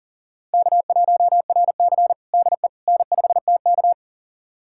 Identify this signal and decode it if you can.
Morse code, heard as a keyed tone, about 40 wpm, 700 Hz. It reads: K1RC DE N5TK
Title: USA Callsign DE Callsign - 4 Characters - 40wpm